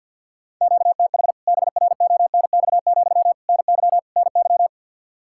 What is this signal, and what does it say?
Morse code, heard as a keyed tone, about 50 wpm, 700 Hz. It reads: QTH BRONX, NY NY
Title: K1USN SST QSO Practice 50wpm